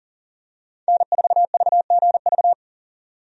Morse code, heard as a keyed tone, about 40 wpm, 700 Hz. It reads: N4VGV
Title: USA Call Signs - 5 Characters - 40wpm